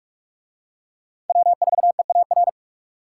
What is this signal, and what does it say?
Morse code, heard as a keyed tone, about 44 wpm, 700 Hz. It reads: W4EAR